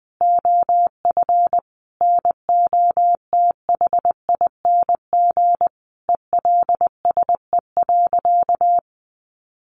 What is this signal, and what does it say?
Morse code, heard as a keyed tone, about 20 wpm, 700 Hz. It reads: OF NOTHING ELSE.
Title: Sentences from Top 800 Words 20wpm